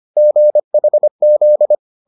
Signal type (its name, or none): Morse code